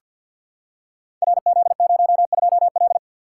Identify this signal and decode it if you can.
Morse code, heard as a keyed tone, about 48 wpm, 700 Hz. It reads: RZ02L
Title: Code Groups - 5 Letter:Numbers - 50wpm